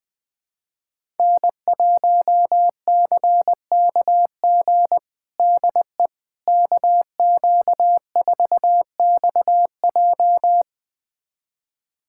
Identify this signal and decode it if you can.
Morse code, heard as a keyed tone, about 20 wpm, 700 Hz. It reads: N1CKG DE KQ4XJ
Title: USA Callsign DE Callsign - 5 Characters - 20wpm